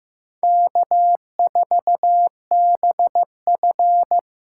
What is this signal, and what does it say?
Morse code, heard as a keyed tone, about 18 wpm, 700 Hz. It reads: K4BF
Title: Speed Racing - USA 1x2 Call Signs 15wpm